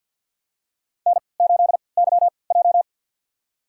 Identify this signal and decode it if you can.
Morse code, heard as a keyed tone, about 53 wpm, spaced as for 29 wpm, 700 Hz. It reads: N8YJ
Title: USA Call Signs - 3x Character Spacing - 50wpm